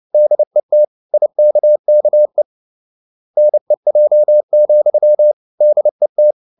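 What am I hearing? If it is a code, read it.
Morse code, heard as a keyed tone, about 29 wpm, 600 Hz. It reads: DET IKKE NEJ, DET
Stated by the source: Fast (QRQ) Morse at 29 wpm, from CW med Gnister.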